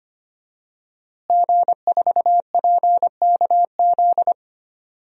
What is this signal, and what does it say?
Morse code, heard as a keyed tone, about 25 wpm, 700 Hz. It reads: G4PKZ
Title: DXCC-100 Call Signs - 5 Characters - Repeated 3x - 25wpm